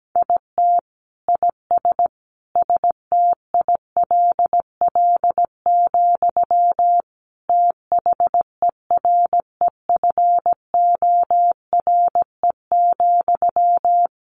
Morse code, heard as a keyed tone, about 17 wpm, 700 Hz. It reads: IT IS STILL, THEREFORE,
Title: Sentences from Top 1000 Words 17wpm